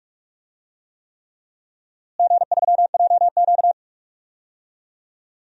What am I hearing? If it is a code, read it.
Morse code, heard as a keyed tone, about 46 wpm, 700 Hz. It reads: G3JQ
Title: DXCC-100 Call Signs - 4 Characters - Repeated 3x - 45wpm